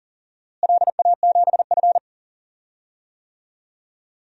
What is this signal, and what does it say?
Morse code, heard as a keyed tone, about 40 wpm, 700 Hz. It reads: LA7F
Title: DXCC-100 Call Signs - 4 Characters - Repeated 3x - 40wpm